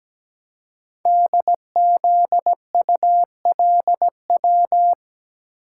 Morse code, heard as a keyed tone, about 17 wpm, 700 Hz. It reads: DZULW
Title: Code Groups - 5 Letters - 17wpm